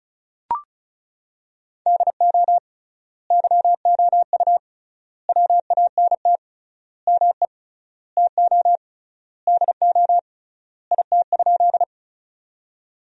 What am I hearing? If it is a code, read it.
Morse code, heard as a keyed tone, about 34 wpm, 700 Hz. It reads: DO YOU WANT ME TO DO IT?
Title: 3x Word-Spacing - Sentences from Top 200 Words - 35wpm